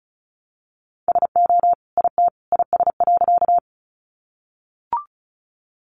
Morse code, heard as a keyed tone, about 34 wpm, 700 Hz. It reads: SO IT IS.
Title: Sentences from Top 100 Words 35wpm